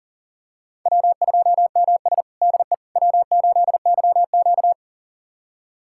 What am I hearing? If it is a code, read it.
Morse code, heard as a keyed tone, about 40 wpm, 700 Hz. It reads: W2MS DE W8YQ
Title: USA Callsign DE Callsign - 4 Characters - 40wpm